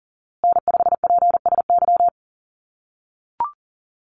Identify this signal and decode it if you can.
Morse code, heard as a keyed tone, about 40 wpm, 700 Hz. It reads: N5PSY
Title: USA Call Signs Repeated 2x 40wpm